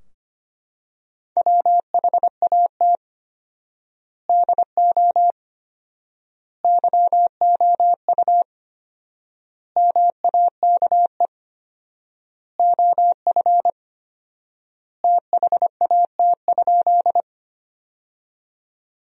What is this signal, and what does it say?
Morse code, heard as a keyed tone, about 25 wpm, 700 Hz. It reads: WHAT DO YOU MAKE OF THAT?
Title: 4x Word-Spacing - Sentences from Top 100 Words - 25wpm